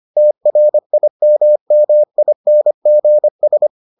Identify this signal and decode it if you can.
Morse code, heard as a keyed tone, about 25 wpm, 600 Hz. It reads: TRIMMINGS